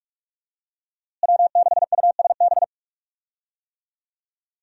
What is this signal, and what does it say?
Morse code, heard as a keyed tone, about 45 wpm, 700 Hz. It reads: W6USB
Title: USA Call Signs - 5 Characters - Repeated 2x - 45wpm